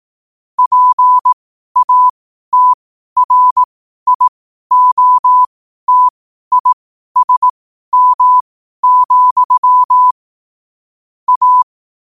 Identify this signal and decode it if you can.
Morse code, heard as a keyed tone, about 18 wpm, spaced as for 12 wpm, 1000 Hz. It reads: PATRIOTISM, A
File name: qotdfiles / 12wpm